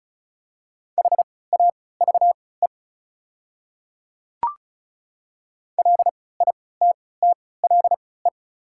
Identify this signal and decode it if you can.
Morse code, heard as a keyed tone, about 36 wpm, spaced as for 20 wpm, 700 Hz. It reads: HAVE LITTLE
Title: 3x Character Spacing - Top 300 Words - 35wpm